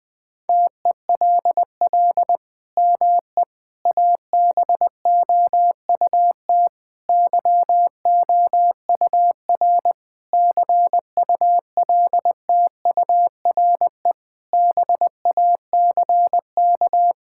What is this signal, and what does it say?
Morse code, heard as a keyed tone, about 21 wpm, 700 Hz. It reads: TELL ME ABOUT YOUR CULTURE BACK
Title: Sentences from Top 1000 Words - Review Part 6 - 20wpm